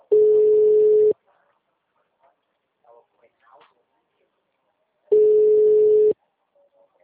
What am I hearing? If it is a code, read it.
Ringback tone (Europe, 425 Hz), with speech.
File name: out- -2009- - - .5193